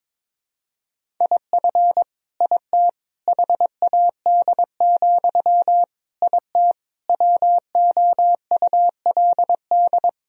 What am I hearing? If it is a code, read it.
Morse code, heard as a keyed tone, about 21 wpm, 700 Hz. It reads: IF IT HAD, IT WOULD
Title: Sentences from Top 1000 Words - Review Part 3 - 22wpm